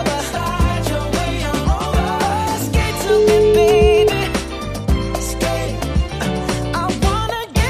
Ringback tone (Europe, 425 Hz), with noise.